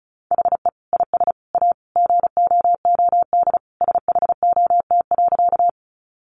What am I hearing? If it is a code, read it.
Morse code, heard as a keyed tone, about 34 wpm, 700 Hz. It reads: HE IS A GOOD SHOT.
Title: Sentences from Top 1100 Words 35wpm